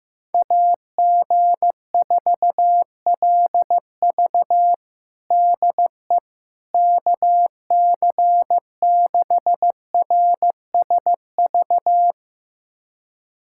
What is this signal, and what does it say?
Morse code, heard as a keyed tone, about 15 wpm, 700 Hz. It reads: AG4LV DE KC6RSV
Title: USA Callsign DE Callsign 15wpm